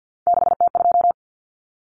Morse code, heard as a keyed tone, about 48 wpm, 700 Hz. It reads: 6T2
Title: Common Cut Numbers - 3-Digit 50wpm